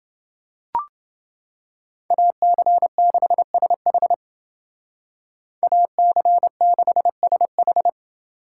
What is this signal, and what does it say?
Morse code, heard as a keyed tone, about 28 wpm, 700 Hz. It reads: AC6SH AC6SH